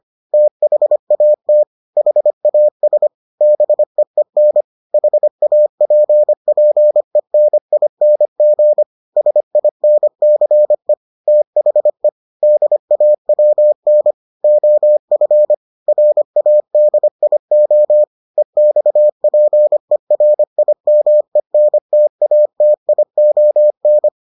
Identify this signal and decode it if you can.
Morse code, heard as a keyed tone, about 25 wpm, 600 Hz. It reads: THAT HAS BEEN HAPPENING SINCE THE DAWN OF RADIO EXPERIMENTATION